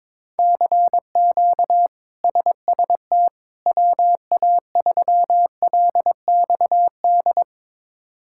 Morse code, heard as a keyed tone, about 22 wpm, 700 Hz. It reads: CQ SST WA3LXD
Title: K1USN SST QSO Practice - 22wpm